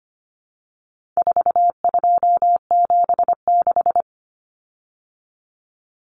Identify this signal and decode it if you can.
Morse code, heard as a keyed tone, about 25 wpm, 700 Hz. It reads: 4276
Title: Common Cut Numbers - 4-Digit 25wpm